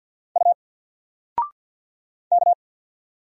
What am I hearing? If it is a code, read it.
Morse code, heard as a keyed tone, about 48 wpm, 700 Hz. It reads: U K
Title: Single Character (Letter:Number:Punctuation:Prosign) Alt 50wpm